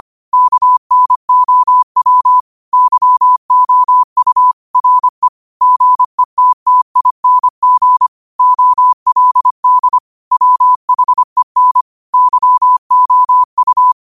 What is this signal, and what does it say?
Morse code, heard as a keyed tone, about 25 wpm, 1000 Hz. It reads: KNOW YOU RE GETTING OLD WHEN YOU
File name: qotdfiles / 25wpm